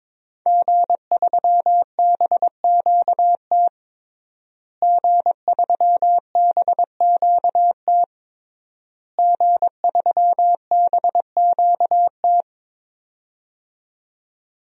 Morse code, heard as a keyed tone, about 22 wpm, 700 Hz. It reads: G3BQT G3BQT G3BQT
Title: DXCC-100 Call Signs - 5 Characters - Repeated 3x - 22wpm